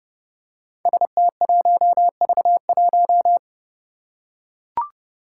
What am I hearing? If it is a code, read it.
Morse code, heard as a keyed tone, about 30 wpm, 700 Hz. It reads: ST1V1
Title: Five Challenging Letter-Number Combinations 30wpm